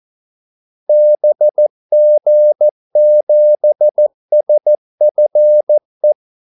Morse code, heard as a keyed tone, about 14 wpm, 600 Hz. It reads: BG7SFE
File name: Callsigns-14wpm